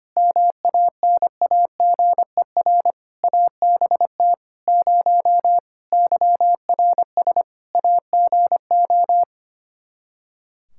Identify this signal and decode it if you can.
Morse code, heard as a keyed tone, about 25 wpm, 700 Hz. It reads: MANAGER ABT 0 YRS AGO
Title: Practice QSO - Exchange 3 - More Info - 25wpm